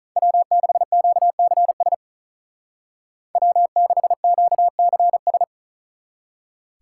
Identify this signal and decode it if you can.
Morse code, heard as a keyed tone, about 39 wpm, 700 Hz. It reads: W6QCS W6QCS
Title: Speed Racing - USA 1x3 Call Signs 35wpm